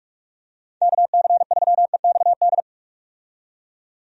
Morse code, heard as a keyed tone, about 46 wpm, 700 Hz. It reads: KC3EXD
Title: USA Call Signs - 6 Characters - 45wpm